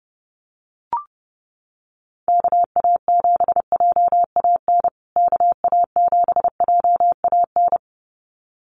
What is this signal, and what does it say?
Morse code, heard as a keyed tone, about 30 wpm, 700 Hz. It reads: KA7JAN KA7JAN
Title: USA Call Signs Repeated 2x 30wpm